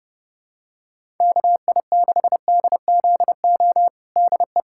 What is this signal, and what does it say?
Morse code, heard as a keyed tone, about 32 wpm, 700 Hz. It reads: KI6DZO DE
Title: USA Callsign DE Callsign 30wpm